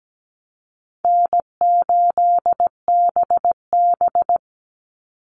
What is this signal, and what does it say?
Morse code, heard as a keyed tone, about 17 wpm, 700 Hz. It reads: N8BB
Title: NA SOTA Chaser Callsigns 17wpm